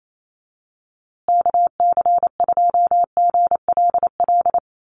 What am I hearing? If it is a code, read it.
Morse code, heard as a keyed tone, about 28 wpm, 700 Hz. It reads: KC2GLL
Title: USA Call Signs Repeated 3x 28wpm